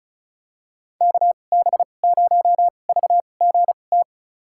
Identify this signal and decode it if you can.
Morse code, heard as a keyed tone, about 36 wpm, spaced as for 25 wpm, 700 Hz. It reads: KB0VGT
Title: USA Call Signs - 6 Characters - 2x Character Spacing - 35wpm